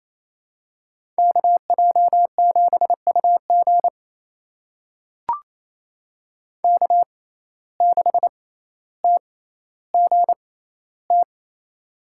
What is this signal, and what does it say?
Morse code, heard as a keyed tone, about 28 wpm, spaced as for 14 wpm, 700 Hz. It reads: KJ7UG K6TGT
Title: USA Call Signs - 5 Characters - 6x Character Spacing - 28wpm